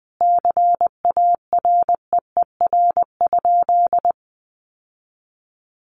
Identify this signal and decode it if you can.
Morse code, heard as a keyed tone, about 20 wpm, 700 Hz. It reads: CAREER?
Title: Sentences from Top 1000 Words 20wpm